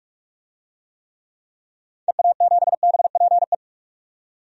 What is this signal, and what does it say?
Morse code, heard as a keyed tone, about 41 wpm, 700 Hz. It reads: EA7BPE
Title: DXCC-100 Call Signs - 6 Characters - 45wpm